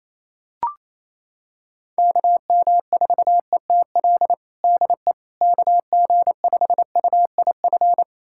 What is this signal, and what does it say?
Morse code, heard as a keyed tone, about 28 wpm, 700 Hz. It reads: KM4ETL DE KG5UIF
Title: USA Callsign DE Callsign - 6 Characters - 28wpm